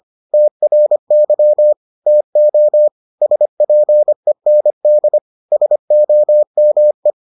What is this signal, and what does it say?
Morse code, heard as a keyed tone, about 25 wpm, 600 Hz. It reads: TRY TO SPEND SOME